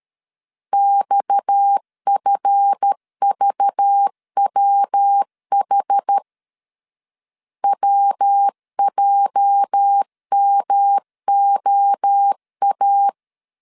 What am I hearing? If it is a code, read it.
Morse code, heard as a keyed tone, about 13 wpm, 800 Hz. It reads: XFVWH WJMOA
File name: LEZIONI / 8 YJ45 / 8 YJ 12wpm 2235